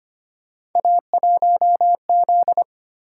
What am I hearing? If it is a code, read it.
Morse code, heard as a keyed tone, about 25 wpm, 700 Hz. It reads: A1Z
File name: Three Letter-Number Combinations - 25wpm